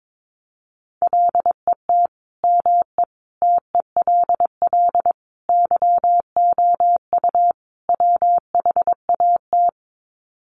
Morse code, heard as a keyed tone, about 22 wpm, 700 Hz. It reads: LET ME TELL YOU WHAT
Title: Sets of 5 Words Encore 22wpm